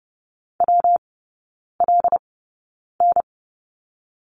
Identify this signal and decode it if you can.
Morse code, heard as a keyed tone, about 30 wpm, 700 Hz. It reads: W L N
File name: Lesson 19e - Letter w - Three Characters 30wpm